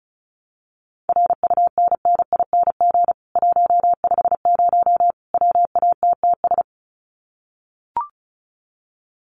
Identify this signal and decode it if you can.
Morse code, heard as a keyed tone, about 34 wpm, 700 Hz. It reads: RUNNING 150 WATTS
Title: Common QSO Phrases 35wpm